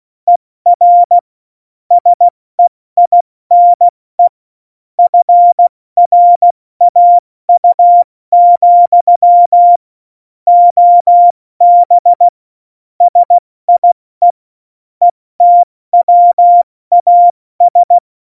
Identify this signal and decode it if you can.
Morse code, heard as a keyed tone, about 16 wpm, 700 Hz. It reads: ER SEINE FRAU, OB SIE ETWAS
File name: Morsekurs / Witze / Witz 21. Januar 2020, 70 F80